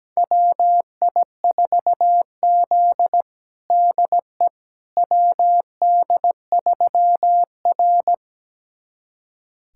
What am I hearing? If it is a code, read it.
Morse code, heard as a keyed tone, about 17 wpm, 700 Hz. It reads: WI4Z DE WD3R
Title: USA Callsign DE Callsign - 4 Characters - 17wpm